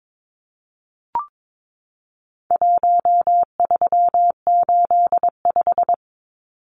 Morse code, heard as a keyed tone, about 22 wpm, 700 Hz. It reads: 1385
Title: Numbers - 4-Digit 22wpm